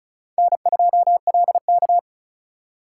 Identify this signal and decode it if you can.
Morse code, heard as a keyed tone, about 34 wpm, 700 Hz. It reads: N2LK
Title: K1USN SST QSO Practice - 35wpm